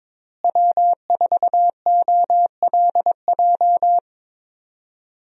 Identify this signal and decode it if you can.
Morse code, heard as a keyed tone, about 23 wpm, 700 Hz. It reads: W4OLJ
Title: USA Call Signs - 5 Characters - Repeated 2x - 22wpm